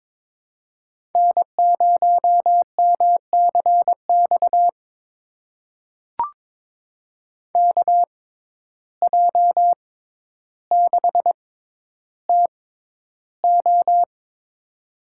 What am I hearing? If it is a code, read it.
Morse code, heard as a keyed tone, about 22 wpm, spaced as for 11 wpm, 700 Hz. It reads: N0MCX KJ6TO
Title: USA Call Signs - 5 Characters - 6x Character Spacing - 22wpm